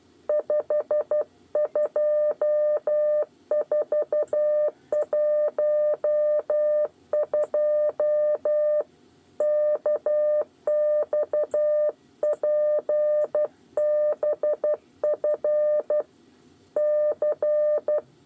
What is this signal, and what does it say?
Morse code, heard as a keyed tone, about 12 wpm, 600 Hz. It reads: 52412 KXPBF C
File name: 8 bis lezione cw 2